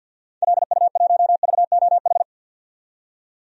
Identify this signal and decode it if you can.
Morse code, heard as a keyed tone, about 46 wpm, 700 Hz. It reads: LU1VOH
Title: DXCC-100 Call Signs - 6 Characters - 50wpm